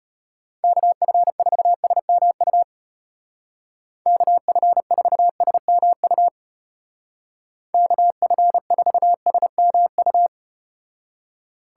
Morse code, heard as a keyed tone, about 32 wpm, 700 Hz. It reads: KF4SMU KF4SMU KF4SMU